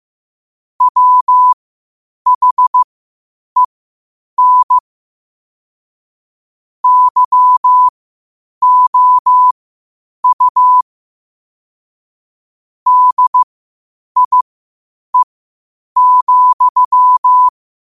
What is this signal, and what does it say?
Morse code, heard as a keyed tone, about 15 wpm, spaced as for 8 wpm, 1000 Hz. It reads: WHEN YOU DIE,
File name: qotdfiles / 8wpm